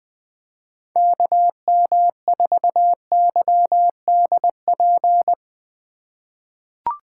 Morse code, heard as a keyed tone, about 20 wpm, 700 Hz. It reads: KM4YDP
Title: USA Call Signs - 6 Characters - 2x Character Spacing - 20wpm